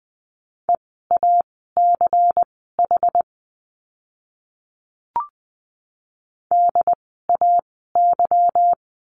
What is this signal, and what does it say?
Morse code, heard as a keyed tone, about 20 wpm, spaced as for 14 wpm, 700 Hz. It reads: EACH DAY
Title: 244 Lesson 39f - Letter z - Words 20wpm